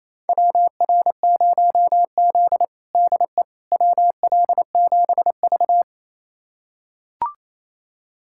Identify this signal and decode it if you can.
Morse code, heard as a keyed tone, about 28 wpm, 700 Hz. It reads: WR0Z DE WL7V
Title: USA Callsign DE Callsign - 4 Characters - 28wpm